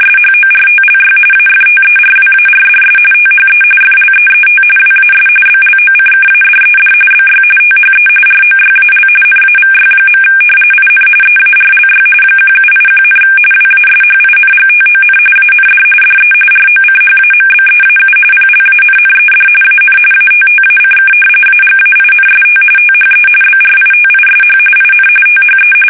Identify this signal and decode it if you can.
RTTY, about 45.45 baud. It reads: SHORTLY AFTERWARDS THE FLEET WAS BLOCKADED BY THE ROYAL NAVY. IN EARLY 1781, WASHINGTON AND DE ROCHAMBEAU PLANNED AN ATTACK AGAINST THE BRITISH IN THE
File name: fsk850 / usnavy1